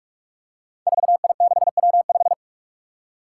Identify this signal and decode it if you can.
Morse code, heard as a keyed tone, about 46 wpm, 700 Hz. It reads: 4I6W5
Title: Five Challenging Letter-Number Combinations 45wpm